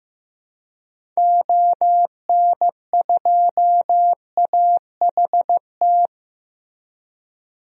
Morse code, heard as a keyed tone, about 15 wpm, 700 Hz. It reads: ON2AHT
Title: DXCC-100 Call Signs - 6 Characters - 15wpm